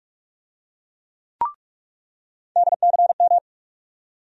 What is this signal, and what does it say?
Morse code, heard as a keyed tone, about 42 wpm, 700 Hz. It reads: DCM
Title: USA Call Sign Postfixes - 3 Letters 45wpm